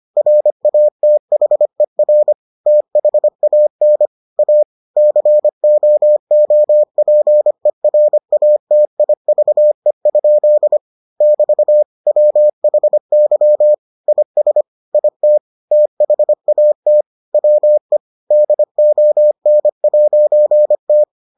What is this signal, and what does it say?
Morse code, heard as a keyed tone, about 25 wpm, 600 Hz. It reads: RATHER THAN A COOPERATIVE? <BT> WHY IS IT THAT WE DON'T